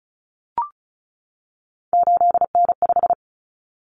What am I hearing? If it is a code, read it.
Morse code, heard as a keyed tone, about 34 wpm, 700 Hz. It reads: 8N5
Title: Common Cut Numbers - 3-Digit 35wpm